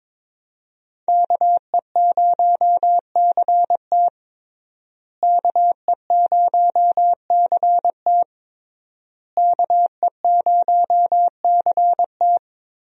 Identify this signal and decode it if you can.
Morse code, heard as a keyed tone, about 22 wpm, 700 Hz. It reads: KE0CT KE0CT KE0CT
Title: USA Call Signs - 5 Characters - Repeated 3x - 22wpm